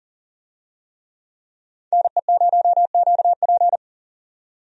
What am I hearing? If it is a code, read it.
Morse code, heard as a keyed tone, about 40 wpm, 700 Hz. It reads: NE0QP
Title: USA Call Signs - 5 Characters - 40wpm